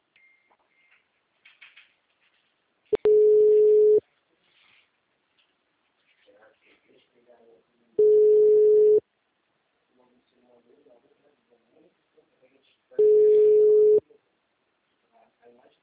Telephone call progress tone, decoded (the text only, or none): ringback tone